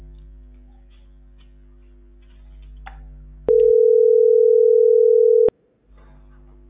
Ringback tone (North America), 440 + 480 Hz.